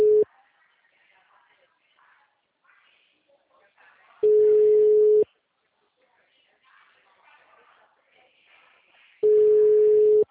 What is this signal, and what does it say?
Ringback tone (Europe, 425 Hz), with speech.